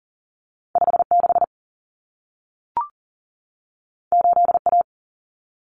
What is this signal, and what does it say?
Morse code, heard as a keyed tone, about 40 wpm, 700 Hz. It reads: 56 8A
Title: Common Cut Numbers - 2-Digit 40wpm